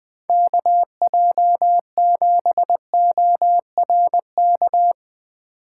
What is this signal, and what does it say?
Morse code, heard as a keyed tone, about 20 wpm, 700 Hz. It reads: KJ7ORK
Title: 2023 POTA Activator Call Signs - 20wpm